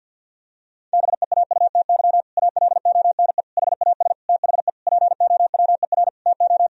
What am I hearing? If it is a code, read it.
Morse code, heard as a keyed tone, about 51 wpm, 700 Hz. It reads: BEAUTY ALONE HAS THE POWER TO
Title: Sentences from Top 1000 Words - Review Part 2 - 50wpm